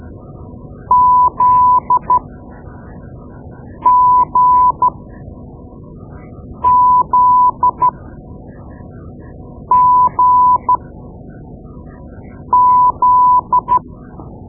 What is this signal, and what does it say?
Morse code, heard as a keyed tone, about 13 wpm, spaced as for 4 wpm, 1000 Hz. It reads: ZGZGZ
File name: Morse Lessons / Morse Lessons / Tape-2B